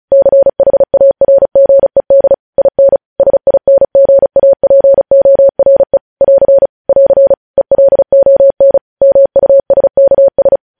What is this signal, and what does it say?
Morse code, heard as a keyed tone, about 36 wpm, 550 Hz. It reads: CHARGED IN SINGAPORE <AR> <AR> ELON MUSKS
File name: podcast / News headlines-1x-35x35